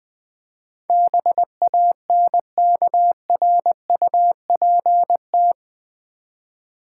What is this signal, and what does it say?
Morse code, heard as a keyed tone, about 20 wpm, 700 Hz. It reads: BANKRUPT
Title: Build A Word - 4 Letters At A Time - 10-Letter Words - 20wpm